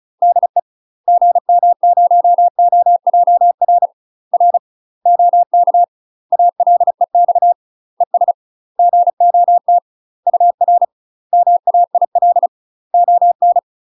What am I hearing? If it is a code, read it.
Morse code, heard as a keyed tone, about 35 wpm, 700 Hz. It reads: DE GM0OJR R OK ALEX ES GOT UR MAIL ON
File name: UK MASTER QSOs TRAINING / 35WPM / UK MASTER QSOs 0005